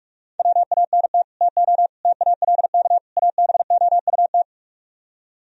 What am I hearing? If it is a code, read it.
Morse code, heard as a keyed tone, about 48 wpm, 700 Hz. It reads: WANT TO TALK ABOUT